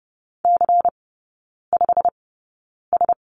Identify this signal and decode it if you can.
Morse code, heard as a keyed tone, about 30 wpm, spaced as for 9 wpm, 690 Hz. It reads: C5S
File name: Lesson 17e - Letter u - Three Characters 30wpm